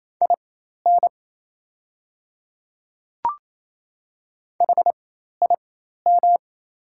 Morse code, heard as a keyed tone, about 28 wpm, spaced as for 13 wpm, 700 Hz. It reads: IN HIM
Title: 4x Character Spacing - Top 100 Words - 28wpm